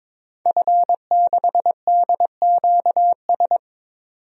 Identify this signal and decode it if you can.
Morse code, heard as a keyed tone, about 22 wpm, 700 Hz. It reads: F6DQS
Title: DXCC-100 Call Signs - 5 Characters - 22wpm